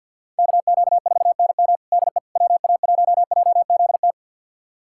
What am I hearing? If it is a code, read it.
Morse code, heard as a keyed tone, about 50 wpm, 700 Hz. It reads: KQ4NM DE WA1JZT
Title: USA Callsign DE Callsign 50wpm